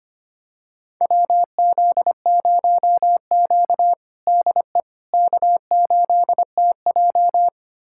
Morse code, heard as a keyed tone, about 25 wpm, 700 Hz. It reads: WZ0Q DE K8TJ
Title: USA Callsign DE Callsign - 4 Characters - 25wpm